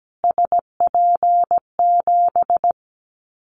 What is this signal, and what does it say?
Morse code, heard as a keyed tone, about 17 wpm, 700 Hz. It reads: SP7
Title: Callsign Prefixes for Top 25 DXCC - Repeated 3x - 17wpm